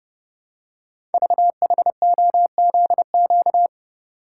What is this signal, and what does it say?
Morse code, heard as a keyed tone, about 30 wpm, 690 Hz. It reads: VHOZQ
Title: Code Groups - 5 Letters - 30wpm